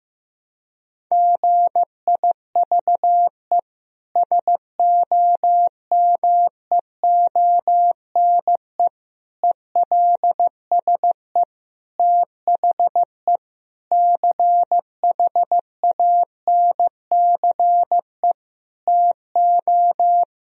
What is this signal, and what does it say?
Morse code, heard as a keyed tone, about 15 wpm, 700 Hz. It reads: GIVE SOMEONE ELSE THE CHANCE TO
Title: Sentences from Top 1000 Words - Review Part 4 - 15wpm